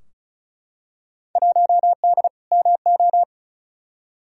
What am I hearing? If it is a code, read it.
Morse code, heard as a keyed tone, about 34 wpm, 700 Hz. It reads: 1D MO
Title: Field Day Exchanges - 35wpm